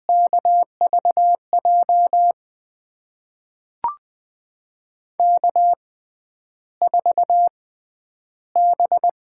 Morse code, heard as a keyed tone, about 20 wpm, spaced as for 10 wpm, 700 Hz. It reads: KVJ K4B
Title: USA Call Signs - 5 Characters - 6x Character Spacing - 20wpm